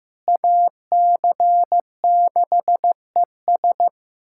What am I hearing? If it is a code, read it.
Morse code, heard as a keyed tone, about 16 wpm, 700 Hz. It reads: AC6ES
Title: USA Callsign DE Callsign - 5 Characters - 15wpm